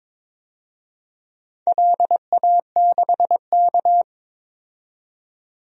Morse code, heard as a keyed tone, about 22 wpm, 700 Hz. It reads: LA6K
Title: DXCC-100 Call Signs - 4 Characters - 22wpm